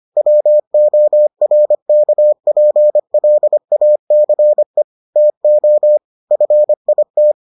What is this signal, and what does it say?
Morse code, heard as a keyed tone, about 25 wpm, 600 Hz. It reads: WORKPLACE TO FIT